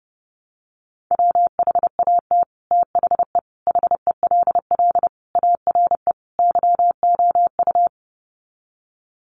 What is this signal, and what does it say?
Morse code, heard as a keyed tone, about 30 wpm, 700 Hz. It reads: WHAT THE HELL ARE YOU
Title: Sets of 5 Words Encore 30wpm